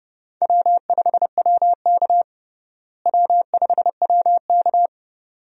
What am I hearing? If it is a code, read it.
Morse code, heard as a keyed tone, about 30 wpm, 700 Hz. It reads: W5WK W5WK
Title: USA Call Signs - 4 Characters - Repeated 2x - 30wpm